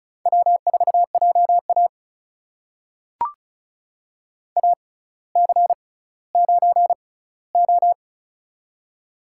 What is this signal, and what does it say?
Morse code, heard as a keyed tone, about 36 wpm, spaced as for 18 wpm, 700 Hz. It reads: W4JA AC9O
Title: USA Call Signs - 4 Characters - 6x Character Spacing - 35wpm